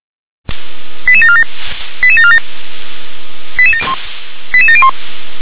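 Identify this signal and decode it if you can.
Five-tone selective call (ZVEI), heard as quick runs of five tones, tones of about 70 ms. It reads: 88747 88747 88701
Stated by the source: Received on SDR, ZVEI five-tone selective call 88081.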